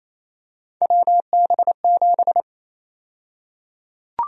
Morse code, heard as a keyed tone, about 28 wpm, 700 Hz. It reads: WB7
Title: USA Call Sign Prefixes - Repeated 2x - 28wpm